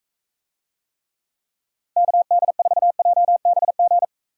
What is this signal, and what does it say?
Morse code, heard as a keyed tone, about 45 wpm, 700 Hz. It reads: KD4JBG